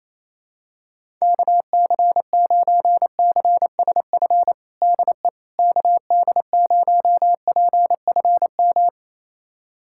Morse code, heard as a keyed tone, about 28 wpm, 700 Hz. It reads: KC9CSF DE KD0PFM
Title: USA Callsign DE Callsign - 6 Characters - 28wpm